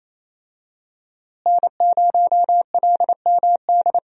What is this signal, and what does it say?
Morse code, heard as a keyed tone, about 28 wpm, 700 Hz. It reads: N0LMD
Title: USA Call Signs - 5 Characters - Repeated 2x - 28wpm